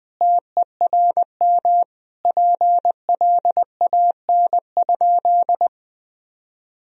Morse code, heard as a keyed tone, about 20 wpm, 700 Hz. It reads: TERM PLAN?
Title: Sentences from Top 1000 Words - Review Part 4 - 20wpm